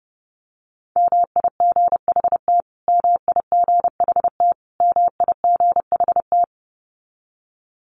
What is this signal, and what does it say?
Morse code, heard as a keyed tone, about 30 wpm, 700 Hz. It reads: MIGHT MIGHT MIGHT
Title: Top 500 Words Repeated 30wpm